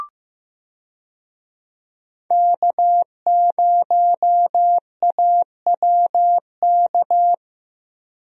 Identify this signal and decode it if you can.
Morse code, heard as a keyed tone, about 15 wpm, 700 Hz. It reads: K0AWK
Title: USA Call Signs - 5 Characters - Repeated 3x - 15wpm